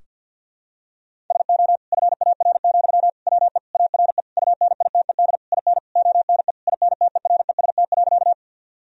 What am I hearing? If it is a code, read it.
Morse code, heard as a keyed tone, about 52 wpm, 700 Hz. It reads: SO FAR, WE ARE UNITED IN ONE INTEREST.
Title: Sentences from Top 1000 Words - Review Part 4 - 50wpm